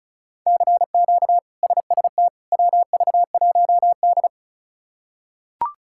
Morse code, heard as a keyed tone, about 36 wpm, 700 Hz. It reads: CQ SST WV1D
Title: K1USN SST QSO Practice - 35wpm